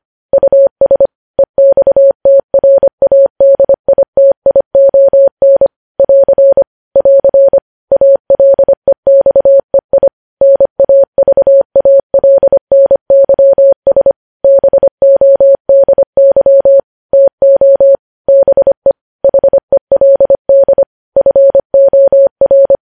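Morse code, heard as a keyed tone, about 25 wpm, 550 Hz. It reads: US EXTRADITION <AR> <AR> ALEXEI NAVALNYS BODY TO BE HELD FOR
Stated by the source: Morse news headlines at 25 WPM.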